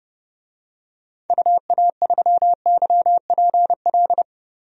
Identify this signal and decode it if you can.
Morse code, heard as a keyed tone, about 30 wpm, 700 Hz. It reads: UA3YPL
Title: DXCC-100 Call Signs - 6 Characters - 30wpm